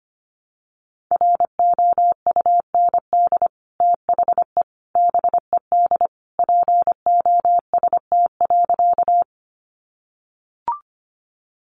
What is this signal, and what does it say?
Morse code, heard as a keyed tone, about 25 wpm, 700 Hz. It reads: ROUND THE BED POST.
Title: Sentences from Top 1000 Words 25wpm